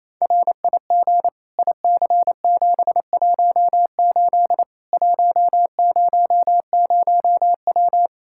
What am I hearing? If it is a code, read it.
Morse code, heard as a keyed tone, about 28 wpm, 700 Hz. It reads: RIG IC718 100W
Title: Practice QSO - Exchange 2 - Info - 28wpm